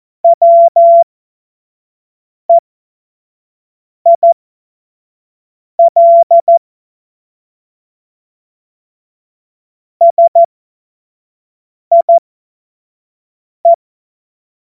Morse code, heard as a keyed tone, about 14 wpm, spaced as for 5 wpm, 680 Hz. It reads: WEIL SIE
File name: Morsekurs / Witze / Witz 17. Januar 2023 25 F70